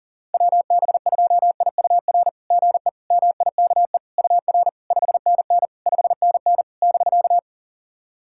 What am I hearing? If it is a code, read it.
Morse code, heard as a keyed tone, about 40 wpm, 700 Hz. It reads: WB2IUR GE MIKE UR 5NN 5NN <BK>
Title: SOTA QSO Practice 40wpm